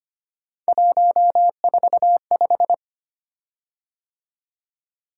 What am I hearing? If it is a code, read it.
Morse code, heard as a keyed tone, about 25 wpm, 700 Hz. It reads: 145
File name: Three Letter-Number Combinations - 25wpm